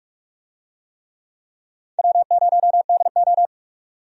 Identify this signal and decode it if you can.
Morse code, heard as a keyed tone, about 42 wpm, 700 Hz. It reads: W0DO
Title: USA Call Signs - 4 Characters - 45wpm